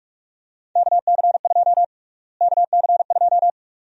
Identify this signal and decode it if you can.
Morse code, heard as a keyed tone, about 42 wpm, 700 Hz. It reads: KC2 KC2
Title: USA Call Sign Prefixes - Repeated 2x - 45wpm